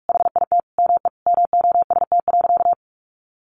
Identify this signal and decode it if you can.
Morse code, heard as a keyed tone, about 44 wpm, 700 Hz. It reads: HIT ME MOST.
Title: Sentences from Top 1400 Words 45wpm